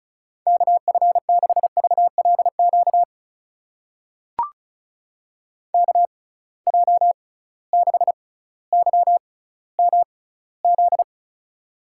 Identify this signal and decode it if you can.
Morse code, heard as a keyed tone, about 36 wpm, spaced as for 18 wpm, 700 Hz. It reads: KF6VLQ KJ6YMZ
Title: USA Call Signs - 6 Characters - 6x Character Spacing - 35wpm